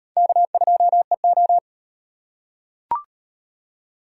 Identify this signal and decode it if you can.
Morse code, heard as a keyed tone, about 38 wpm, 700 Hz. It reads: K2EO